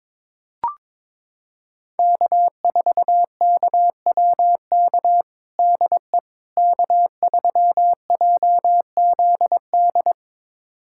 Morse code, heard as a keyed tone, about 22 wpm, 700 Hz. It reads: K4KWK DE K3JZD
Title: USA Callsign DE Callsign - 5 Characters - 22wpm